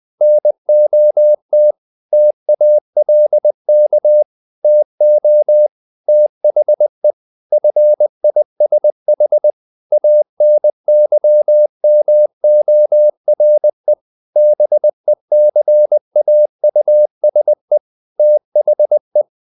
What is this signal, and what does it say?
Morse code, heard as a keyed tone, about 20 wpm, 600 Hz. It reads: NOT TALK TO THE FISH ANYMORE BECAUSE THE
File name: the old man and the sea part 2 20wpm